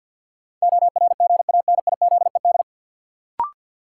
Morse code, heard as a keyed tone, about 48 wpm, 700 Hz. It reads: ORGANIZED
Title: Sentences from Top 1300 Words - New Word Drill - 50wpm